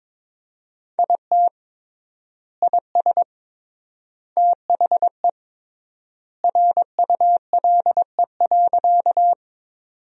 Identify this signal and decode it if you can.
Morse code, heard as a keyed tone, about 22 wpm, 700 Hz. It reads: IT IS THE RULE.
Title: 3x Word-Spacing - Sentences from Top 400 Words - 22wpm